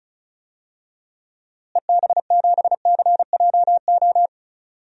Morse code, heard as a keyed tone, about 34 wpm, 700 Hz. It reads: EB7CJO
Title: DXCC-100 Call Signs - 6 Characters - Repeated 3x - 35wpm